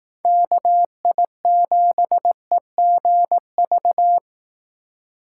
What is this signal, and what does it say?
Morse code, heard as a keyed tone, about 18 wpm, 700 Hz. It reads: KI7EGV
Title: Speed Racing - USA 2x3 Call Signs 15wpm